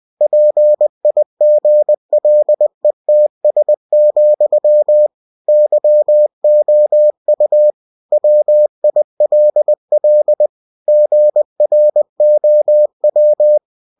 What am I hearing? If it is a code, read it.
Morse code, heard as a keyed tone, about 20 wpm, 600 Hz. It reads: PIGLETS, YOU WILL GROW
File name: cw-books / winnie-the-pooh / winnie-20wpm-page-0072